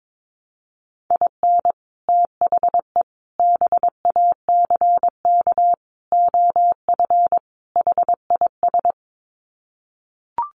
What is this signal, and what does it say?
Morse code, heard as a keyed tone, about 22 wpm, 700 Hz. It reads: IN THE BACK OF HIS
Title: Sets of 5 Words Encore2 22wpm